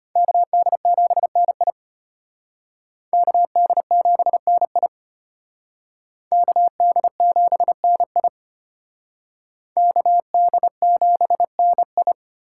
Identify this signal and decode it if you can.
Morse code, heard as a keyed tone, about 32 wpm, 700 Hz. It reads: KD7NI KD7NI KD7NI KD7NI